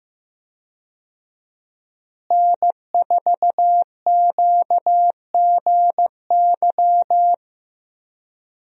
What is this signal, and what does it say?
Morse code, heard as a keyed tone, about 15 wpm, 700 Hz. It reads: N4QGY
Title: USA Call Signs - 5 Characters - Repeated 2x - 15wpm